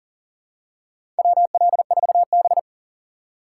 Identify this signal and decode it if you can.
Morse code, heard as a keyed tone, about 40 wpm, 700 Hz. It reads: WL4B
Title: USA Call Signs - 4 Characters - 3x Character Spacing - 40wpm